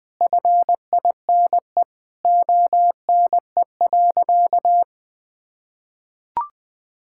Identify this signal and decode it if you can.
Morse code, heard as a keyed tone, about 21 wpm, 700 Hz. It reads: FINE ONE.
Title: Sentences from Top 1000 Words - Review Part 6 - 20wpm